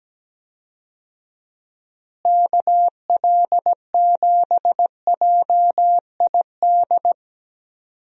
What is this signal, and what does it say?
Morse code, heard as a keyed tone, about 17 wpm, 700 Hz. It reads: KL7JID
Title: DXCC-100 Call Signs - 6 Characters - Repeated 2x - 17wpm